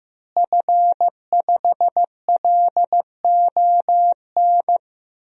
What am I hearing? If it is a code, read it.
Morse code, heard as a keyed tone, about 15 wpm, 700 Hz. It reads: F5LON
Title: DXCC-100 Call Signs - 5 Characters - Repeated 2x - 15wpm